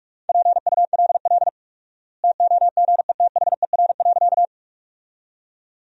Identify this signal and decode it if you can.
Morse code, heard as a keyed tone, about 46 wpm, 700 Hz. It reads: PULL TOGETHER.
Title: 4x Word-Spacing - Sentences from Top 400 Words - 45wpm